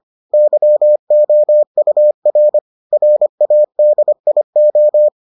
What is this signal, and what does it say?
Morse code, heard as a keyed tone, about 25 wpm, 600 Hz. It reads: YOUR RADIO